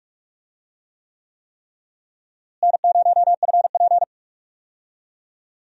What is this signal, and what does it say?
Morse code, heard as a keyed tone, about 46 wpm, 700 Hz. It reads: N0FP
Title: USA Call Signs - 4 Characters - Repeated 3x - 45wpm